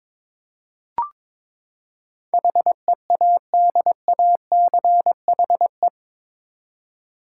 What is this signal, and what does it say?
Morse code, heard as a keyed tone, about 21 wpm, 700 Hz. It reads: HEADACHE
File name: practice / 22wpm / Eight Letter Words - 22wpm